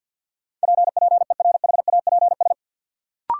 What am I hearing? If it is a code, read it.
Morse code, heard as a keyed tone, about 52 wpm, 700 Hz. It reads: PPERHAPS
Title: Sentences from Top 1100 Words - New Word Drill - 50wpm